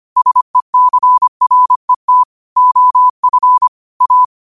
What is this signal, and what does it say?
Morse code, heard as a keyed tone, about 25 wpm, 1000 Hz. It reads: SECRET OF A
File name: qotdfiles / 25wpm